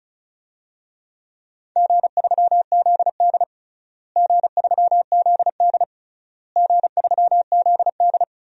Morse code, heard as a keyed tone, about 34 wpm, 700 Hz. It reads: G3ZD G3ZD G3ZD
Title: DXCC-100 Call Signs - 4 Characters - Repeated 3x - 35wpm